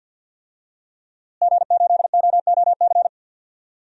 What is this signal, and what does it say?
Morse code, heard as a keyed tone, about 51 wpm, 700 Hz. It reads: G8OOC
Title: DXCC-100 Call Signs - 5 Characters - Repeated 3x - 50wpm